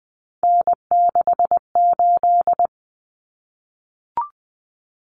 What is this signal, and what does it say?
Morse code, heard as a keyed tone, about 20 wpm, 700 Hz. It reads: N68
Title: Common Cut Numbers - 3-Digit 20wpm